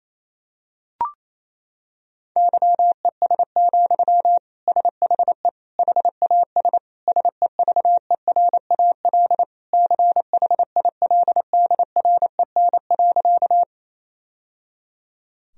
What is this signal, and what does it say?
Morse code, heard as a keyed tone, about 28 wpm, 700 Hz. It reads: YES, SHE HAS SEVERAL CHILDREN.
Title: Sentences from Top 1000 Words - Review Part 2 - 28wpm